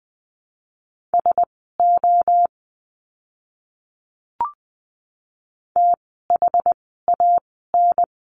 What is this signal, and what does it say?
Morse code, heard as a keyed tone, about 20 wpm, spaced as for 15 wpm, 700 Hz. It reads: SO THAN
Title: 237 Lesson 38f - Letter q - Words 20wpm